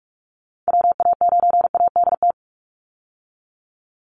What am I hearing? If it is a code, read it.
Morse code, heard as a keyed tone, about 44 wpm, 700 Hz. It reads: WA9ADT
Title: Call Signs Repeated 3x - 45wpm